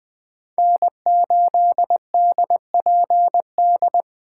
Morse code, heard as a keyed tone, about 21 wpm, 700 Hz. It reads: N8DPD
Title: K1USN SST QSO Practice - 20wpm